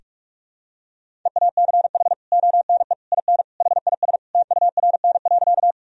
Morse code, heard as a keyed tone, about 45 wpm, 700 Hz. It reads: EACH ONE IN HIS TURN.
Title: Sentences from Top 1000 Words - Review Part 1 - 45wpm